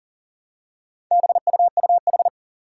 Morse code, heard as a keyed tone, about 37 wpm, 700 Hz. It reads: BUUH
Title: Four Challenging Letter-Number Combinations 40wpm